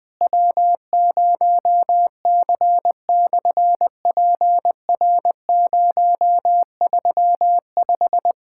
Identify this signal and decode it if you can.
Morse code, heard as a keyed tone, about 20 wpm, 700 Hz. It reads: W0C/PR035
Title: SOTA QSO Practice 20wpm